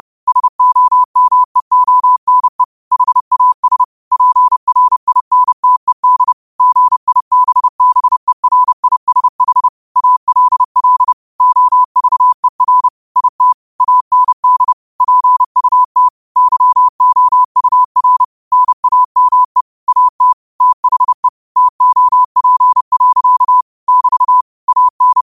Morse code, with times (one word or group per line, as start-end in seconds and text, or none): 0.3-2.6 SOMEONE
2.9-3.8 HAS
4.1-6.3 PRINTED
6.6-9.7 GIBBERISH
10.0-11.1 ALL
11.4-12.9 OVER
13.2-13.5 IT
13.8-14.7 AND
15.0-16.1 PUT
16.4-18.2 YOUR
18.5-19.6 NAME
19.9-20.3 AT
20.6-21.3 THE
21.6-23.6 TOP.
23.9-24.4 <BT>
24.7-25.2 AN